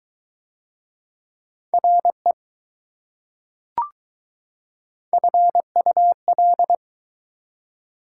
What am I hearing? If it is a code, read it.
Morse code, heard as a keyed tone, about 22 wpm, 700 Hz. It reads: RE FUL